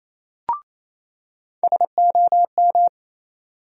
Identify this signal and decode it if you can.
Morse code, heard as a keyed tone, about 28 wpm, 700 Hz. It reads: SOM
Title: USA Call Sign Postfixes - 3 Letters 28wpm